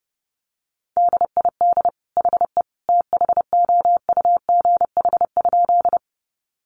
Morse code, heard as a keyed tone, about 30 wpm, 700 Hz. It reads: DID HE THOUGH?
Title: Sentences from Top 400 Words 30wpm